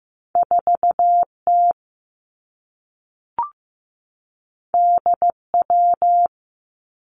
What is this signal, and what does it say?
Morse code, heard as a keyed tone, about 15 wpm, 700 Hz. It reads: 4T DW
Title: Two Challenging Letter-Number Combinations 15wpm